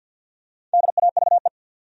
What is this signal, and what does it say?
Morse code, heard as a keyed tone, about 53 wpm, 700 Hz. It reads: DAVE
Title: USA Common Names - Max 5 Letters - 50wpm